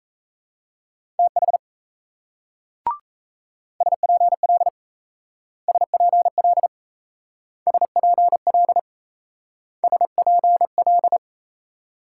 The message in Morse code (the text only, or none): TH SPL SPL SPL SPL